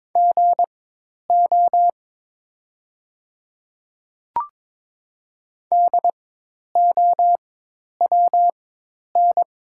Morse code, heard as a keyed tone, about 22 wpm, spaced as for 10 wpm, 700 Hz. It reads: GO DOWN
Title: 4x Character Spacing - Top 100 Words - 22wpm